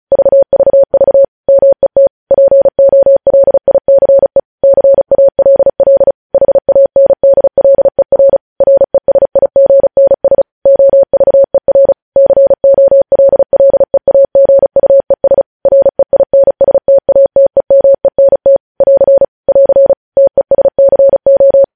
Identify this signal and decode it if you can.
Morse code, heard as a keyed tone, about 36 wpm, 550 Hz. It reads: VVV MET POLICE CALL HANDLER RESIGNS OVER COLLEAGUES REINSTATEMENT <AR> <AR> TESCO
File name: podcast / News headlines-1x-35x35